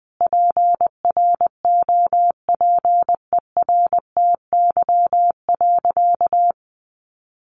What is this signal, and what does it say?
Morse code, heard as a keyed tone, about 20 wpm, 700 Hz. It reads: PROPERTY.
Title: Sentences from Top 1000 Words 20wpm